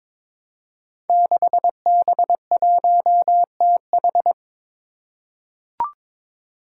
Morse code, heard as a keyed tone, about 22 wpm, 700 Hz. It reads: 6B1TH
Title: Five Challenging Letter-Number Combinations 22wpm